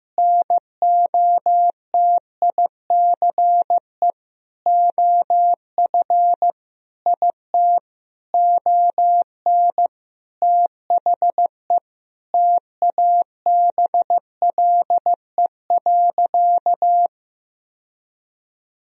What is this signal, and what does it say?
Morse code, heard as a keyed tone, about 15 wpm, 700 Hz. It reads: NOTICE OF IT ON THE TABLE.
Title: Sentences from Top 1000 Words - Review Part 2 - 15wpm